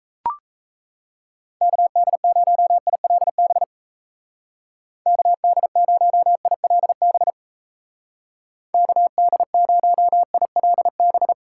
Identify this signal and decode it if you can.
Morse code, heard as a keyed tone, about 35 wpm, 700 Hz. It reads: KD0ILB KD0ILB KD0ILB
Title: Speed Racing - USA 2x3 Call Signs 28wpm